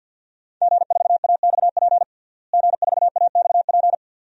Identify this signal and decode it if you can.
Morse code, heard as a keyed tone, about 47 wpm, 700 Hz. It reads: G4AXP G4AXP
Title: DXCC-100 Call Signs - 5 Characters - Repeated 2x - 50wpm